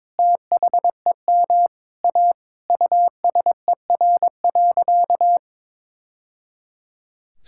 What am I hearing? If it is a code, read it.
Morse code, heard as a keyed tone, about 21 wpm, 700 Hz. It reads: THEM A USER.
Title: Sentences from Top 1000 Words - Review Part 5 - 22wpm